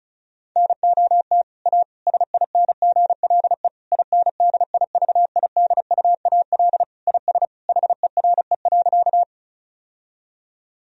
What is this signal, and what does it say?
Morse code, heard as a keyed tone, about 34 wpm, 700 Hz. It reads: NOT A SINGLE INDIVIDUAL IS HERE.
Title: Sentences from Top 1000 Words - Review Part 4 - 35wpm